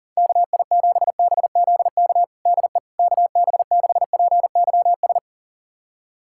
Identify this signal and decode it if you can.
Morse code, heard as a keyed tone, about 37 wpm, 700 Hz. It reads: KI7BZK DE KB6PYS
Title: USA Callsign DE Callsign - 6 Characters - 40wpm